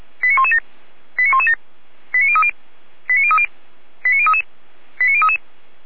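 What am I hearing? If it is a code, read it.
Five-tone selective call (ZVEI), heard as quick runs of five tones, tones of about 70 ms. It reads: 89117 89117 89030 89030 89033 89033
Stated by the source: ZVEI five-tone selective call received via SDR.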